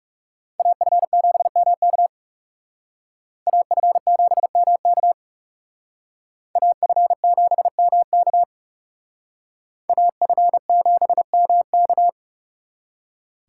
Morse code, heard as a keyed tone, about 36 wpm, 700 Hz. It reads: AF7MK AF7MK AF7MK AF7MK
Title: Speed Racing - USA Call Signs 30wpm